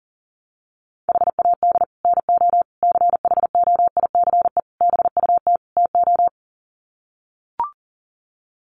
Morse code, heard as a keyed tone, about 40 wpm, 700 Hz. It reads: HAD NO CHOICE BUT TO
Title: Sets of 5 Words 40wpm